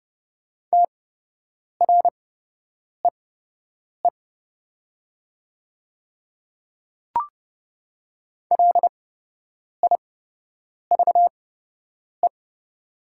Morse code, heard as a keyed tone, about 29 wpm, spaced as for 8 wpm, 700 Hz. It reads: TREE LIVE
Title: 8x Character Spacing - Top 300 Words - 30wpm